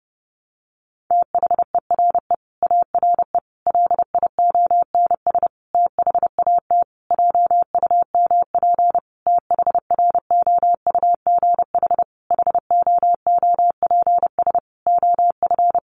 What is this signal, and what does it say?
Morse code, heard as a keyed tone, about 30 wpm, 700 Hz. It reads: THERE ARE LIONS THAT JUMP THROUGH HOOPS OF
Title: Easy Phrases 30wpm